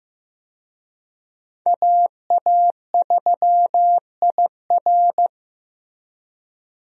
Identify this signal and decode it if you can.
Morse code, heard as a keyed tone, about 15 wpm, 700 Hz. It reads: AA3IR
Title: USA Call Signs - 5 Characters - Repeated 3x - 15wpm